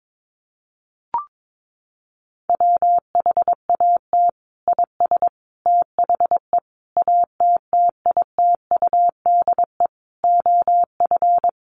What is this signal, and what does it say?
Morse code, heard as a keyed tone, about 22 wpm, 700 Hz. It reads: WHAT IS THE ATTITUDE OF
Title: Sentences from Top 1300 Words 22wpm